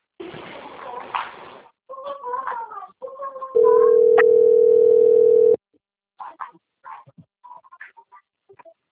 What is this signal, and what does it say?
Ringback tone (North America), 440 + 480 Hz.